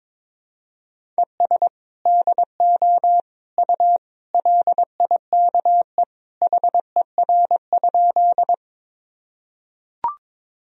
Morse code, heard as a keyed tone, about 22 wpm, 700 Hz. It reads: ES DO U LIKE HER?
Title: Sentences from Top 100 Words and Abbreviations 22wpm